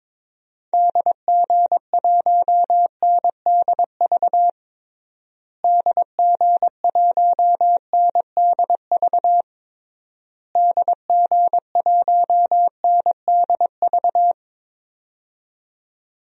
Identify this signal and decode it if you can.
Morse code, heard as a keyed tone, about 22 wpm, 700 Hz. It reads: DG1NDV DG1NDV DG1NDV
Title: DXCC-100 Call Signs - 6 Characters - Repeated 3x - 22wpm